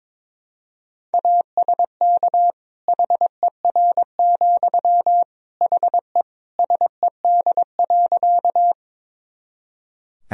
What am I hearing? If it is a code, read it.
Morse code, heard as a keyed tone, about 23 wpm, 700 Hz. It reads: ASK HER, HE SED.
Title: Sentences from Top 200 Words and Abbreviations 22wpm